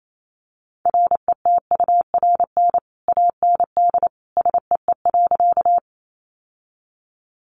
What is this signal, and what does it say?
Morse code, heard as a keyed tone, about 28 wpm, 700 Hz. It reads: RETURN AND SEE.
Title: Sentences from Top 800 Words 28wpm